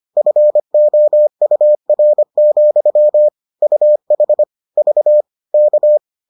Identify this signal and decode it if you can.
Morse code, heard as a keyed tone, about 25 wpm, 600 Hz. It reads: FOUR, UH V K